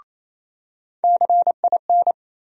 Morse code, heard as a keyed tone, about 28 wpm, 700 Hz. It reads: CIN
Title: USA Call Sign Postfixes - 3 Letters 28wpm